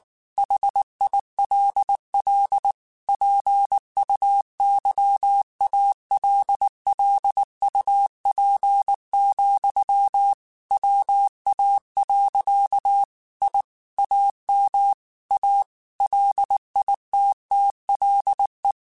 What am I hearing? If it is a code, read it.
Morse code, heard as a keyed tone, about 19 wpm, 800 Hz. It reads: HILL PUYALLUP, WA. I AM A LITTLE
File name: QSOs / QSOs-mp3s / WA7QOX 2@18